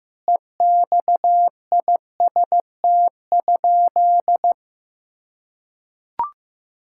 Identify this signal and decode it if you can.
Morse code, heard as a keyed tone, about 14 wpm, 700 Hz. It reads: EXIST?
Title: Sentences from Top 1000 Words - Review Part 5 - 15wpm